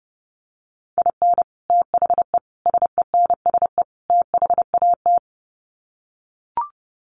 Morse code, heard as a keyed tone, about 30 wpm, 700 Hz. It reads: IN THE SENSE THAT
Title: Sets of 4 Words 30wpm